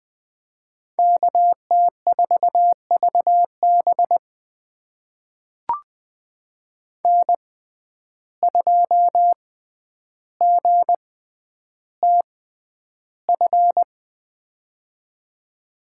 Morse code, heard as a keyed tone, about 21 wpm, spaced as for 10 wpm, 700 Hz. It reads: KT4VB N2GTF
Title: USA Call Signs - 5 Characters - 6x Character Spacing - 20wpm